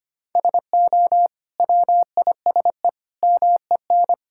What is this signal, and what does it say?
Morse code, heard as a keyed tone, about 25 wpm, 700 Hz. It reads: SO WISE MEN
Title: Sentences from Top 1600 Words 25wpm